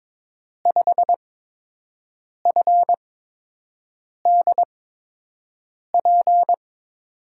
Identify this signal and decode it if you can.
Morse code, heard as a keyed tone, about 22 wpm, spaced as for 6 wpm, 700 Hz. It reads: 5FDP
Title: USA Call Signs - 5 Characters - 8x Character Spacing - 22wpm